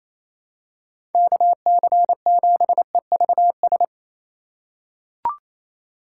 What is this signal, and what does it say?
Morse code, heard as a keyed tone, about 28 wpm, 700 Hz. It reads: KC7EVS
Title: USA Call Signs - 6 Characters - 2x Character Spacing - 28wpm